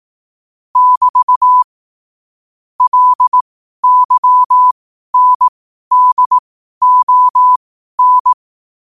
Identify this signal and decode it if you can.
Morse code, heard as a keyed tone, about 18 wpm, spaced as for 12 wpm, 1000 Hz. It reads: <BT> LYNDON
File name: qotdfiles / 12wpm